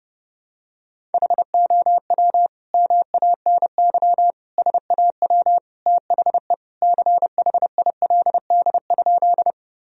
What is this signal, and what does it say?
Morse code, heard as a keyed tone, about 30 wpm, 700 Hz. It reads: HOW MANY SAW THE CHILD?
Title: Sentences from Top 1000 Words - Review Part 1 - 30wpm